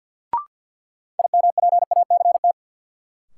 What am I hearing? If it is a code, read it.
Morse code, heard as a keyed tone, about 51 wpm, 700 Hz. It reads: IMPACT
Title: Sentences from Top 1000 Words - Review Part 6 - New Word Drill - 50wpm